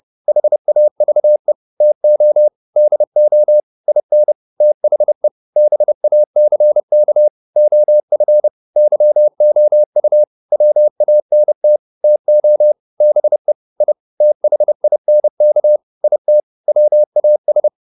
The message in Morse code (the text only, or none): HAVE TO DO IN THE BACK OF YOU WANT TO BE I THINK IT WAS